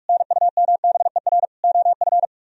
Morse code, heard as a keyed tone, about 58 wpm, 700 Hz. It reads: NUMBER OF